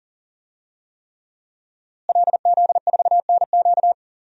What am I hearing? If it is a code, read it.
Morse code, heard as a keyed tone, about 40 wpm, 700 Hz. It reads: LZ4NQ
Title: DXCC-100 Call Signs - 5 Characters - Repeated 3x - 40wpm